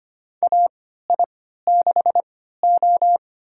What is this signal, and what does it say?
Morse code, heard as a keyed tone, about 25 wpm, spaced as for 14 wpm, 700 Hz. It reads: AI6O
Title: USA Call Signs - 4 Characters - 3x Character Spacing - 25wpm